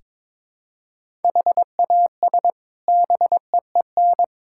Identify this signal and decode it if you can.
Morse code, heard as a keyed tone, about 22 wpm, 700 Hz. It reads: HAS BEEN
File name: practice / 22wpm / Sets of 7 Words - 22wpm